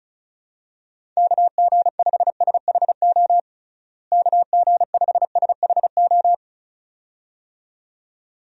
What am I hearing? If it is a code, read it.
Morse code, heard as a keyed tone, about 34 wpm, 700 Hz. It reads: KG5SHO KG5SHO
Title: USA Call Signs - 6 Characters - Repeated 2x - 35wpm